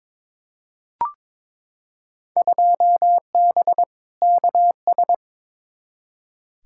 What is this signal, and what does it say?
Morse code, heard as a keyed tone, about 22 wpm, 700 Hz. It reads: 2B KS
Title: Field Day Exchanges Encore - 22wpm